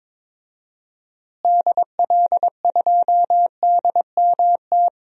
Morse code, heard as a keyed tone, about 22 wpm, 700 Hz. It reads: DL2DMT
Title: DXCC-100 Call Signs - 6 Characters - Repeated 2x - 22wpm